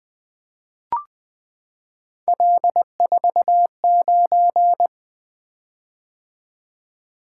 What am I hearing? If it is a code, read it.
Morse code, heard as a keyed tone, about 20 wpm, 700 Hz. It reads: L49
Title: Three Letter-Number Combinations - 20wpm